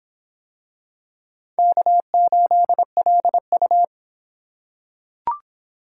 Morse code, heard as a keyed tone, about 24 wpm, 700 Hz. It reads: K8LU